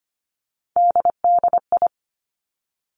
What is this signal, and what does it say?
Morse code, heard as a keyed tone, about 25 wpm, 700 Hz. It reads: DDI
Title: Three Challenging Letter-Number Combinations 25wpm